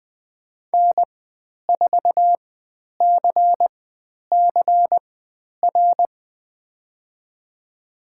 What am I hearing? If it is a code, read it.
Morse code, heard as a keyed tone, about 20 wpm, spaced as for 10 wpm, 700 Hz. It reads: N4CCR
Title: 095 Lesson 16g - Letter c - Call Signs 20wpm